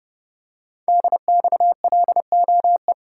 Morse code, heard as a keyed tone, about 29 wpm, 690 Hz. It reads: DXLOE
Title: Code Groups - 5 Letters - 30wpm